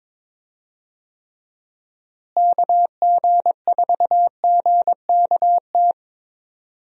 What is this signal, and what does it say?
Morse code, heard as a keyed tone, about 22 wpm, 700 Hz. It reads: KG4GKT
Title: USA Call Signs - 6 Characters - Repeated 2x - 22wpm